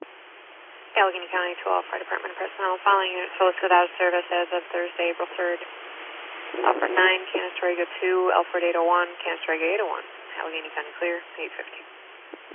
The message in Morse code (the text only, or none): none